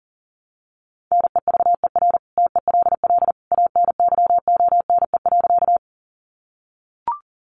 Morse code, heard as a keyed tone, about 40 wpm, 700 Hz. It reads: NEVER TELL ANYONE.
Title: Sentences from Top 900 Words 40wpm